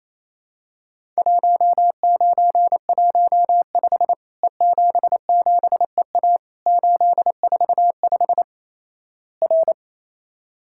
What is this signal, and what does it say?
Morse code, heard as a keyed tone, about 28 wpm, 700 Hz. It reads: 1915 E77EA 845 R
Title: Worked All Europe DX Contest - QTC Exchange - 28wpm